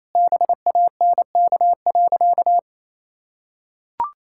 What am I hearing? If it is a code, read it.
Morse code, heard as a keyed tone, about 28 wpm, 700 Hz. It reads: BANK.
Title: Sentences from Top 1000 Words - Review Part 6 - 28wpm